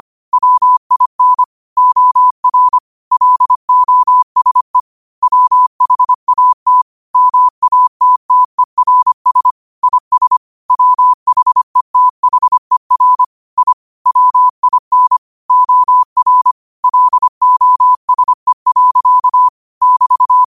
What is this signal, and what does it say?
Morse code, heard as a keyed tone, about 25 wpm, 1000 Hz. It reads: WIN OR LOSE WHAT MATTERS IS WHETHER I WIN OR LOSE. <BT>
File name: qotdfiles / 25wpm